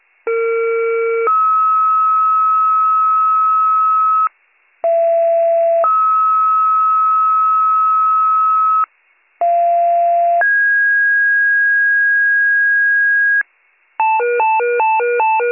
Two-tone page, heard as a one-second tone, then a three-second tone, about 460 then 1290 Hz, about 670 then 1290 Hz, about 690 then 1650 Hz, with noise. It is heard from 0.3 to 13.4 s.